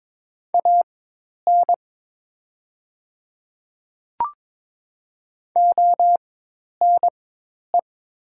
Morse code, heard as a keyed tone, about 22 wpm, spaced as for 10 wpm, 700 Hz. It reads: AN ONE
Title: 4x Character Spacing - Top 100 Words - 22wpm